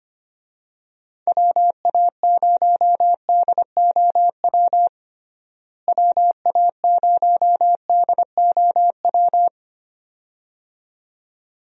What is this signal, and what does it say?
Morse code, heard as a keyed tone, about 25 wpm, 700 Hz. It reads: WA0DOW WA0DOW
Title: USA Call Signs - 6 Characters - Repeated 2x - 25wpm